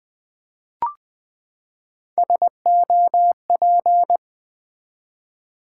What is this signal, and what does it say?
Morse code, heard as a keyed tone, about 20 wpm, 700 Hz. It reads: SOP
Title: Build A Word - 3 Letters At A Time - 9-Letter Words - 20wpm